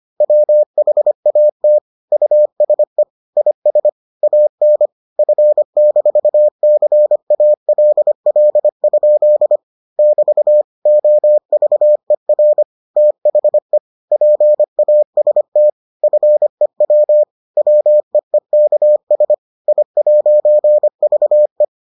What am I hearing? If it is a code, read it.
Morse code, heard as a keyed tone, about 25 wpm, 600 Hz. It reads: WHAT USE IS AN F-CALL? <BT> OVER THE PAST FEW WEEKS I'VE